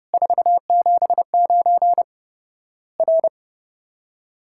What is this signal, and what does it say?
Morse code, heard as a keyed tone, about 30 wpm, 700 Hz. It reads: 479 R
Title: Worked All Europe DX Contest - QTC Exchange - 30wpm